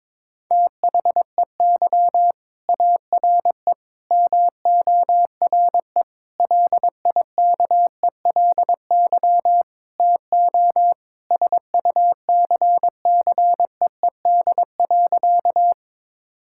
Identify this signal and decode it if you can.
Morse code, heard as a keyed tone, about 22 wpm, 700 Hz. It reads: THEY ARE MORE LIKELY TO SUCCEED.
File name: practice / 22wpm / Sentences with 4-Word Phrases - 22wpm